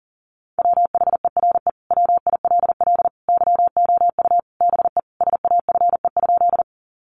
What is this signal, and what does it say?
Morse code, heard as a keyed tone, about 40 wpm, 700 Hz. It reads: WHERE WILL YOU BE SAFE?
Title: Sentences from Top 900 Words 40wpm